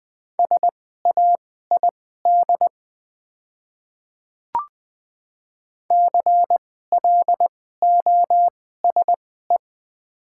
Morse code, heard as a keyed tone, about 20 wpm, spaced as for 14 wpm, 700 Hz. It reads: SAID CLOSE
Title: 2x Character Spacing - Top 300 Words - 20wpm